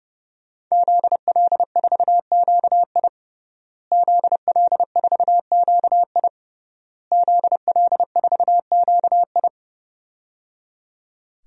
Morse code, heard as a keyed tone, about 30 wpm, 700 Hz. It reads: ZL4QI ZL4QI ZL4QI
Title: DXCC-100 Call Signs - 5 Characters - Repeated 3x - 30wpm